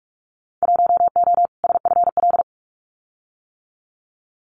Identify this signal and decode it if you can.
Morse code, heard as a keyed tone, about 46 wpm, 700 Hz. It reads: 1O SFL
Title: Winter Field Day Exchanges Encore 45wpm